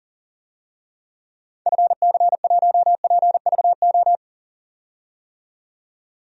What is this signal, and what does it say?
Morse code, heard as a keyed tone, about 40 wpm, 700 Hz. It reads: FC1PVO
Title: DXCC-100 Call Signs - 6 Characters - 40wpm